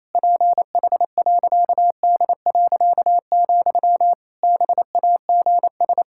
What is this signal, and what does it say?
Morse code, heard as a keyed tone, about 28 wpm, 700 Hz. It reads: PH.D., BAGS
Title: Pangram Challenge v2 - 28wpm